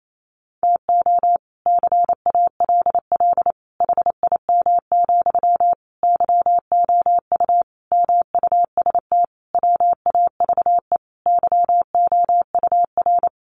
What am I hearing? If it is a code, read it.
Morse code, heard as a keyed tone, about 28 wpm, 700 Hz. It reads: TO CALL HIM, YOU MUST WAVE YOUR
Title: Sentences from Top 1400 Words 28wpm